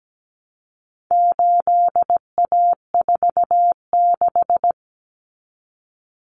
Morse code, heard as a keyed tone, about 17 wpm, 700 Hz. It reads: 8A46
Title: Common Cut Numbers - 4-Digit 17wpm